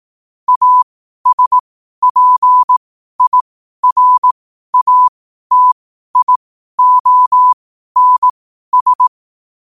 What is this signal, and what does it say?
Morse code, heard as a keyed tone, about 18 wpm, spaced as for 12 wpm, 1000 Hz. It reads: ASPIRATIONS
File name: qotdfiles / 12wpm